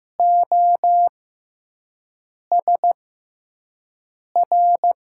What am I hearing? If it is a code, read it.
Morse code, heard as a keyed tone, about 15 wpm, spaced as for 5 wpm, 700 Hz. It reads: OSR
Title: USA Call Signs - 6 Characters - 6x Character Spacing - 15wpm